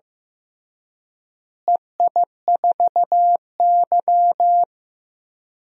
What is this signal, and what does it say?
Morse code, heard as a keyed tone, about 15 wpm, 700 Hz. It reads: EI4Y
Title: DXCC-100 Call Signs - 4 Characters - Repeated 2x - 15wpm